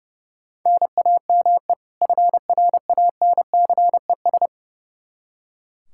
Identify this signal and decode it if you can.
Morse code, heard as a keyed tone, about 30 wpm, 700 Hz. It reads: NAME FRANCES
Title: Practice QSO - Exchange 1 - Beginning - 30wpm